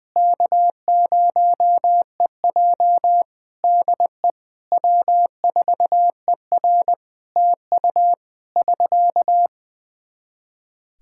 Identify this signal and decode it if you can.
Morse code, heard as a keyed tone, about 20 wpm, 700 Hz. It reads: K0EJ DE W4ER TU <SK>
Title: Practice QSO - Exchange 4 - Ending - 20wpm